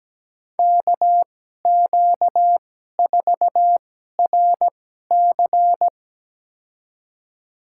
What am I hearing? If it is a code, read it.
Morse code, heard as a keyed tone, about 17 wpm, spaced as for 12 wpm, 700 Hz. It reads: KQ4RC
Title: USA Call Signs - 5 Characters - 2x Character Spacing - 17wpm